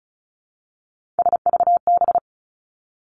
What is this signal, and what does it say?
Morse code, heard as a keyed tone, about 36 wpm, 700 Hz. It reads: SVB
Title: Three Challenging Letter-Number Combinations 35wpm